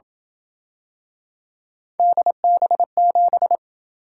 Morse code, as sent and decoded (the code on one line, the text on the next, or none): -.. -... --...
DB7